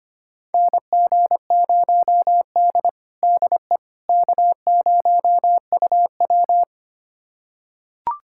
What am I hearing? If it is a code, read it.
Morse code, heard as a keyed tone, about 25 wpm, 700 Hz. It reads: NG0D DE K0UW
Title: USA Callsign DE Callsign - 4 Characters - 25wpm